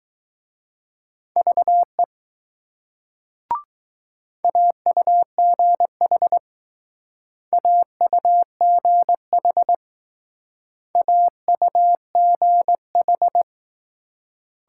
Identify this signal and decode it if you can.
Morse code, heard as a keyed tone, about 20 wpm, 700 Hz. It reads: VE AUGH AUGH AUGH